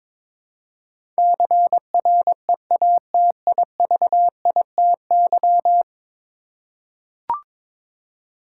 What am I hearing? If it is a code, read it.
Morse code, heard as a keyed tone, about 22 wpm, 700 Hz. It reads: CREATIVITY
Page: morse-code-ninja-practice-22wpm